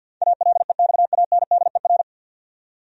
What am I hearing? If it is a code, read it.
Morse code, heard as a keyed tone, about 52 wpm, 700 Hz. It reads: ALEXANDER
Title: USA Common Names - 50wpm